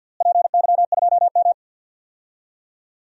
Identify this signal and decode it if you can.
Morse code, heard as a keyed tone, about 45 wpm, 700 Hz. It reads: PY2M
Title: DXCC-100 Call Signs - 4 Characters - Repeated 2x - 50wpm